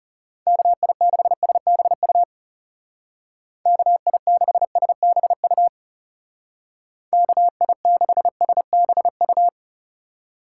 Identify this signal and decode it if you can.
Morse code, heard as a keyed tone, about 35 wpm, 700 Hz. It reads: KI6SBU KI6SBU KI6SBU
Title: Speed Racing - USA 2x3 Call Signs 30wpm